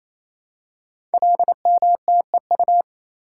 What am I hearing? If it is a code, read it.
Morse code, heard as a keyed tone, about 27 wpm, 700 Hz. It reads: LMTEU
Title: Code Groups - 5 Letters - 28wpm